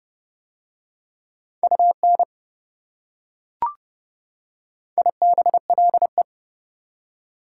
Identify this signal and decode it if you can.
Morse code, heard as a keyed tone, about 28 wpm, 700 Hz. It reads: UN IBLE